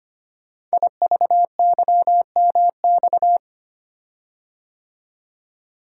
Morse code, heard as a keyed tone, about 25 wpm, 700 Hz. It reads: IVYMX
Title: Code Groups - 5 Letters - 25wpm